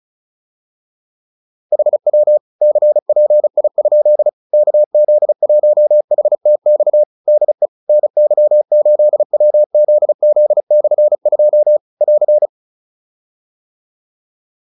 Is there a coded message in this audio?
Yes: Morse code from 1.7 to 12.5 s, heard as a keyed tone, about 34 wpm, 600 Hz. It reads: HW CPI? KZ1HTX DE NY8WZZ/2 <AR>